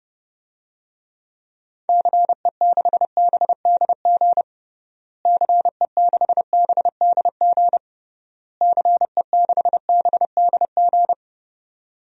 Morse code, heard as a keyed tone, about 31 wpm, 700 Hz. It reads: CE6BDG CE6BDG CE6BDG
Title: DXCC-100 Call Signs - 6 Characters - Repeated 3x - 30wpm